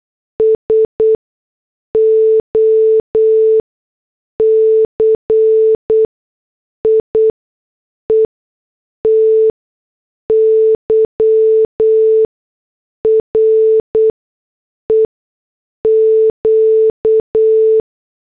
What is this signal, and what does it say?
Morse code, heard as a keyed tone, about 8 wpm, spaced as for 6 wpm, 440 Hz. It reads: SOCIETY REQ